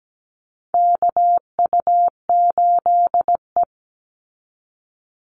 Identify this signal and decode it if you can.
Morse code, heard as a keyed tone, about 17 wpm, 700 Hz. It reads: KU8E
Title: Call Signs Repeated 3x - 17wpm